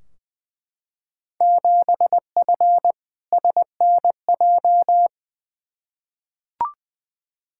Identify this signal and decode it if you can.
Morse code, heard as a keyed tone, about 21 wpm, 700 Hz. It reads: 7F SNJ
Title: Field Day Exchanges Encore - 20wpm